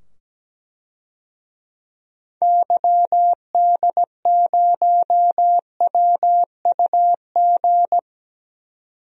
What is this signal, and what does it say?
Morse code, heard as a keyed tone, about 17 wpm, 700 Hz. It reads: YD0WUG
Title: DXCC-100 Call Signs - 6 Characters - 17wpm